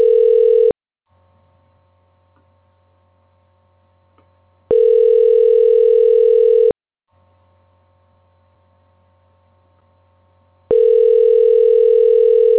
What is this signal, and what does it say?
Ringback tone (North America), 440 + 480 Hz.